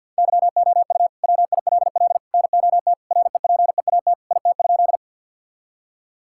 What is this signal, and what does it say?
Morse code, heard as a keyed tone, about 50 wpm, 700 Hz. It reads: YOU WILL NOT REPEAT IT?
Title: Sentences from Top 1600 Words 50wpm